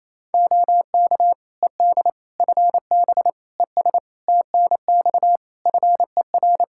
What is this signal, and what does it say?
Morse code, heard as a keyed tone, about 27 wpm, 700 Hz. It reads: OK ED FB ES TNX FER
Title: K1USN SST QSO Practice 28wpm